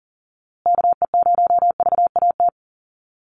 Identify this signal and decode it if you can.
Morse code, heard as a keyed tone, about 40 wpm, 700 Hz. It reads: KE0VAT
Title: USA Call Signs Repeated 2x 40wpm